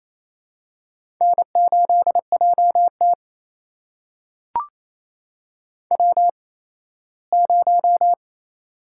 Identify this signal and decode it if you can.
Morse code, heard as a keyed tone, about 29 wpm, spaced as for 17 wpm, 700 Hz. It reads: N8JT W0
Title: USA Call Signs - 4 Characters - 8x Character Spacing - 28wpm